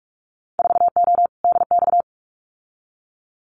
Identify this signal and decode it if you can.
Morse code, heard as a keyed tone, about 46 wpm, 700 Hz. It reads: 4O DX
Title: Winter Field Day Exchanges Encore 45wpm